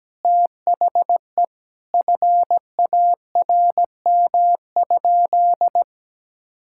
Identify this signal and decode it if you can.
Morse code, heard as a keyed tone, about 17 wpm, 700 Hz. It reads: THE FARM?
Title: Sentences from Top 1000 Words - Review Part 6 - 17wpm